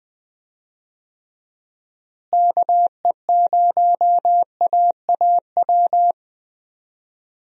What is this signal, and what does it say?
Morse code, heard as a keyed tone, about 20 wpm, 700 Hz. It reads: KE0AAW
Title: USA Call Signs - 6 Characters - Repeated 2x - 20wpm